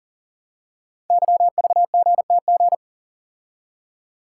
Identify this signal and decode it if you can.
Morse code, heard as a keyed tone, about 41 wpm, 700 Hz. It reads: YVGTG
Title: Code Groups - 5 Letters - 40wpm